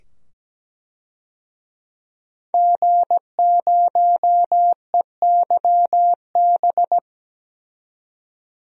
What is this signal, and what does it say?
Morse code, heard as a keyed tone, about 17 wpm, 700 Hz. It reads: G0EYB
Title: DXCC-100 Call Signs - 5 Characters - 17wpm